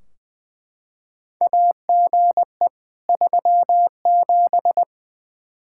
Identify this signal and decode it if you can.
Morse code, heard as a keyed tone, about 20 wpm, 700 Hz. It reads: AGE 37
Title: K1USN SST QSO Practice 20wpm